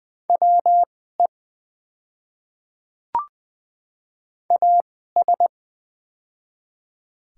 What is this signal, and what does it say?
Morse code, heard as a keyed tone, about 20 wpm, spaced as for 14 wpm, 700 Hz. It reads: WE AS
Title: 2x Character Spacing - Top 100 Words - 20wpm